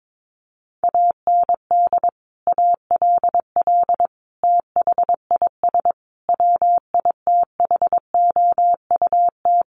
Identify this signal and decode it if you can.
Morse code, heard as a keyed tone, about 22 wpm, 700 Hz. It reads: AND ALL THIS WITHOUT
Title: Sentences from Top 1200 Words 22wpm